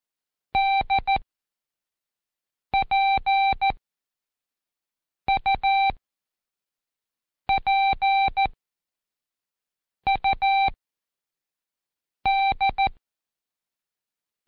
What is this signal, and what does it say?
Morse code, heard as a keyed tone, about 14 wpm, spaced as for 5 wpm, 780 Hz. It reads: DPUPUD
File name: Code Quick / AudioDownld1 / Seg24-Day3-Intro P